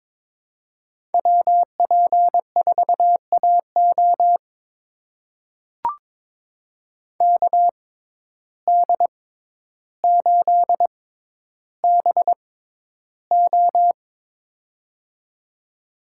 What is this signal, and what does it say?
Morse code, heard as a keyed tone, about 22 wpm, spaced as for 11 wpm, 700 Hz. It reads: WP4AO KD8BO
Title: USA Call Signs - 5 Characters - 6x Character Spacing - 22wpm